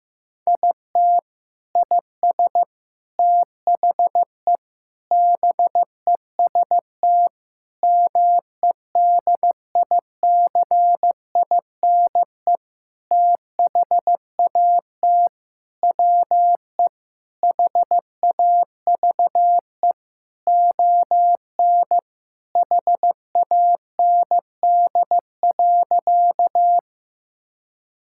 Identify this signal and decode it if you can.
Morse code, heard as a keyed tone, about 15 wpm, 700 Hz. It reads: IT IS THE BEST MEDICINE THAT WE HAVE ON HAND.
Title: Sentences from Top 1000 Words - Review Part 6 - 15wpm